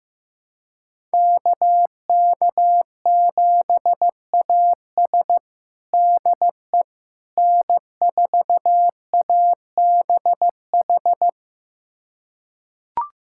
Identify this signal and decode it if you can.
Morse code, heard as a keyed tone, about 15 wpm, 700 Hz. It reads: KK7AS DE N4ABH
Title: USA Callsign DE Callsign - 5 Characters - 15wpm